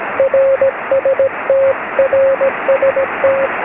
Morse code, heard as a keyed tone, about 17 wpm, 540 Hz, with radio noise. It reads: RST RST
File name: sm7lcb / 17-oz6ol